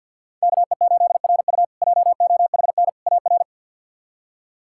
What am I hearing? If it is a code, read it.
Morse code, heard as a keyed tone, about 52 wpm, 700 Hz. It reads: KE8RU JOHN AR
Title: K1USN SST QSO Practice - 50wpm